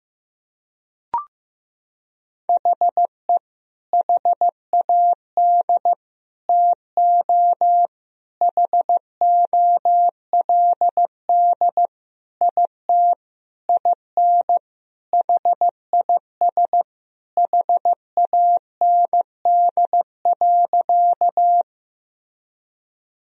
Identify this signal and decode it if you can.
Morse code, heard as a keyed tone, about 16 wpm, 700 Hz. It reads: HE HAD TO HOLD IT IN HIS HAND.
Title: Sentences from Top 1000 Words - Review Part 2 - 15wpm